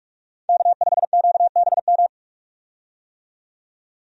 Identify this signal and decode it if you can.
Morse code, heard as a keyed tone, about 45 wpm, 700 Hz. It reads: K5QBM
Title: USA Call Signs - 5 Characters - 45wpm